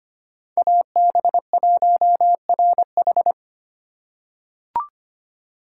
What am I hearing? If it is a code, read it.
Morse code, heard as a keyed tone, about 25 wpm, 700 Hz. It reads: AB1RH
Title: USA Call Signs - 5 Characters - 2x Character Spacing - 25wpm